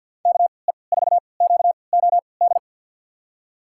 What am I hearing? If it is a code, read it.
Morse code, heard as a keyed tone, about 50 wpm, spaced as for 28 wpm, 700 Hz. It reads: KE4QOD
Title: USA Call Signs - 6 Characters - 3x Character Spacing - 50wpm